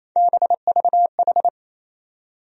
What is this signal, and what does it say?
Morse code, heard as a keyed tone, about 28 wpm, 700 Hz. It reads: BVH
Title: USA Call Sign Postfixes - 3 Letters 28wpm